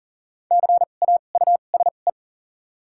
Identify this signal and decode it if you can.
Morse code, heard as a keyed tone, about 40 wpm, spaced as for 29 wpm, 700 Hz. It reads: CAUSE
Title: 2x Character Spacing - Top 200 Words - 40wpm